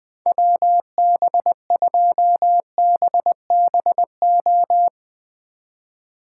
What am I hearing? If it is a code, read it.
Morse code, heard as a keyed tone, about 22 wpm, 700 Hz. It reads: WB2BBO
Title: Speed Racing - USA 2x3 Call Signs 20wpm